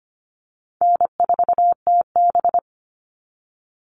Morse code, heard as a keyed tone, about 25 wpm, 700 Hz. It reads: N4TB
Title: Call Signs Repeated 2x - 25wpm